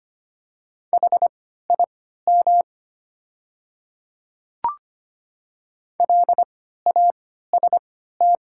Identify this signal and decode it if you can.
Morse code, heard as a keyed tone, about 25 wpm, spaced as for 14 wpm, 700 Hz. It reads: HIM LAST
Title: 3x Character Spacing - Top 500 Words - 25wpm